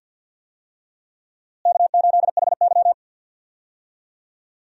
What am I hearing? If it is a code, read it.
Morse code, heard as a keyed tone, about 51 wpm, 700 Hz. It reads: K8HY
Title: USA Call Signs - 4 Characters - Repeated 2x - 50wpm